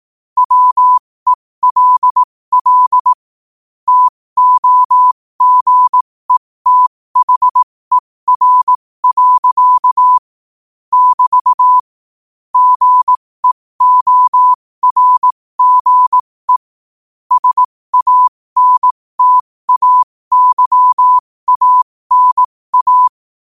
Morse code, heard as a keyed tone, about 18 wpm, spaced as for 15 wpm, 1000 Hz. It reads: WELL TOGETHER. <BT> GEORGE SANTAYANA
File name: qotdfiles / 15wpm